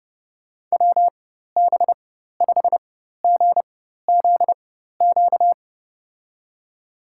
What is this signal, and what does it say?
Morse code, heard as a keyed tone, about 30 wpm, spaced as for 14 wpm, 700 Hz. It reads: WB5GZQ
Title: USA Call Signs - 6 Characters - 4x Character Spacing - 30wpm